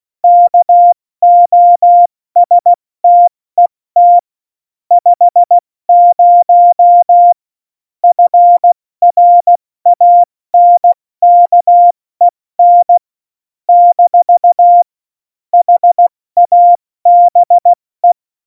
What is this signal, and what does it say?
Morse code, heard as a keyed tone, about 16 wpm, spaced as for 14 wpm, 700 Hz. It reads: KOSTET 50 FRANKEN - HABE
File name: Morsekurs / Witze / Witz 07 Juli 2020, 70 F80